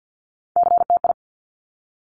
Morse code, heard as a keyed tone, about 50 wpm, 700 Hz. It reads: CTI
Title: Top Three Letter Combinations in Words 50wpm